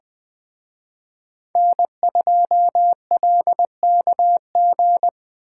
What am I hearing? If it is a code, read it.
Morse code, heard as a keyed tone, about 20 wpm, 700 Hz. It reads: N2LKG
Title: Speed Racing - USA 1x3 Call Signs 17wpm